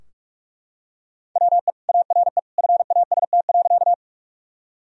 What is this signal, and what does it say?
Morse code, heard as a keyed tone, about 43 wpm, 700 Hz. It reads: WE ARE FAST.
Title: Sentences from Top 1000 Words - Review Part 2 - 45wpm